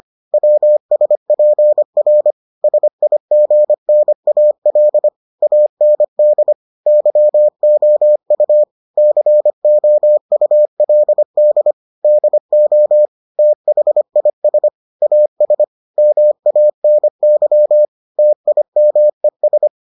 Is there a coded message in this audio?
Yes: Morse code from 0.3 to 19.7 s, heard as a keyed tone, about 25 wpm, 600 Hz. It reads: WSPR SIGNAL AND YOU COULD DO THIS AS MANY TIMES